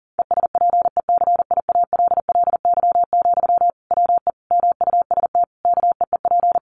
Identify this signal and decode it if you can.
Morse code, heard as a keyed tone, about 40 wpm, 700 Hz. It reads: ESPECIALLY, WE MUST KEEP
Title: Sentences from Top 700 Words 40wpm